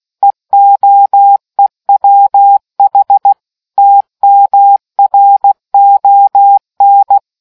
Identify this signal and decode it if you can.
Morse code, heard as a keyed tone, about 16 wpm, 800 Hz. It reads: EOEWH TMRON